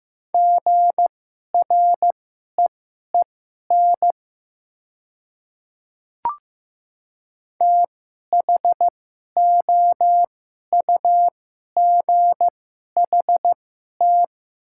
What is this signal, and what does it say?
Morse code, heard as a keyed tone, about 15 wpm, spaced as for 11 wpm, 690 Hz. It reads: GREEN THOUGHT
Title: 2x Character Spacing - Top 500 Words - 15wpm